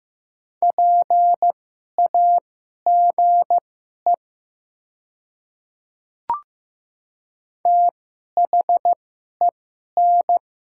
Morse code, heard as a keyed tone, about 15 wpm, spaced as for 11 wpm, 700 Hz. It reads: PAGE THEN
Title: 2x Character Spacing - Top 300 Words - 15wpm